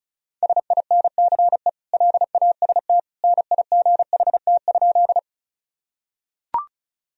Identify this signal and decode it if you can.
Morse code, heard as a keyed tone, about 36 wpm, 700 Hz. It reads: SINCE LAST NIGHT?
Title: Sentences from Top 1000 Words - Review Part 1 - 35wpm